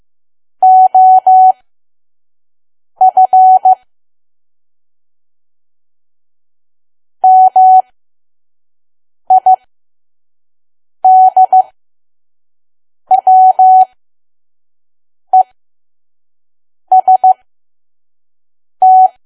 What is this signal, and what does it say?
Morse code, heard as a keyed tone, about 15 wpm, spaced as for 5 wpm, 750 Hz. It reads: OF MIDWEST
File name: Archive / 5 WPM files / PM